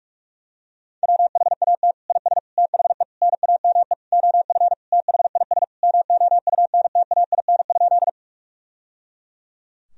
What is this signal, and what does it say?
Morse code, heard as a keyed tone, about 41 wpm, 700 Hz. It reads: WHAT IS THE NAME OF THIS MOUNTAIN?
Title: Sentences from Top 1000 Words - Review Part 1 - 45wpm